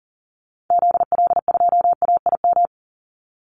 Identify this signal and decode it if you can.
Morse code, heard as a keyed tone, about 40 wpm, 700 Hz. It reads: ZL2AIM
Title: Call Signs Repeated 2x - 40wpm